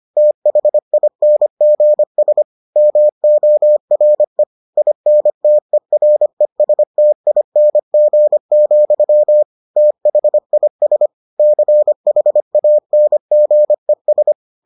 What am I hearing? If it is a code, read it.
Morse code, heard as a keyed tone, about 25 wpm, 600 Hz. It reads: THINGS MORE INTERESTING, THIS CHANGES